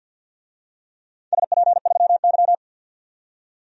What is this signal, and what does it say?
Morse code, heard as a keyed tone, about 46 wpm, 700 Hz. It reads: SP3Y
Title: DXCC-100 Call Signs - 4 Characters - 50wpm